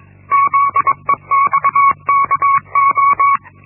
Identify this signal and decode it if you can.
Morse code, heard as a keyed tone, about 21 wpm, 1120 Hz. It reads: ZEXKO
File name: Linksmakalnis-1